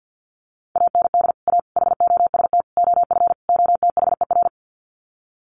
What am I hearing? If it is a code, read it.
Morse code, heard as a keyed tone, about 50 wpm, 700 Hz. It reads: AND A HOST OF OTHER
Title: Sets of 5 Words Encore2 50wpm